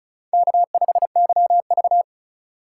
Morse code, heard as a keyed tone, about 46 wpm, 700 Hz. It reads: K5YV